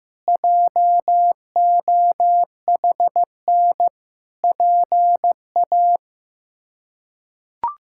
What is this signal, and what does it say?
Morse code, heard as a keyed tone, about 16 wpm, 700 Hz. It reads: JOHN PA
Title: K1USN SST QSO Practice - 15wpm